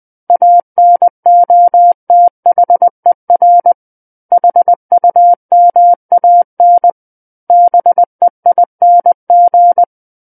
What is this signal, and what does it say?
Morse code, heard as a keyed tone, about 20 wpm, 700 Hz. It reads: ANOTHER HUMAN BEING
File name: 20 WPM